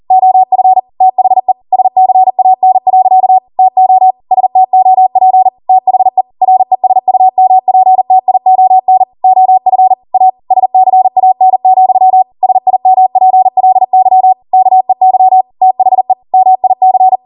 Morse code, heard as a keyed tone, about 40 wpm, 750 Hz. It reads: OF THE SCAN. TO STOP THE RESUMPTION OF A SCAN, SIMPLY KEY THE MIC
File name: Morse / 40WPM